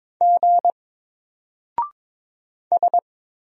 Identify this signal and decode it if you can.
Morse code, heard as a keyed tone, about 22 wpm, 700 Hz. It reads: G S
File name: Single-Letter-Number-22wpm